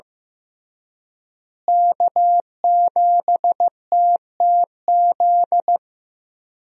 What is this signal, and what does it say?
Morse code, heard as a keyed tone, about 16 wpm, 700 Hz. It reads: K7TTZ
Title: USA Call Signs - 5 Characters - Repeated 3x - 15wpm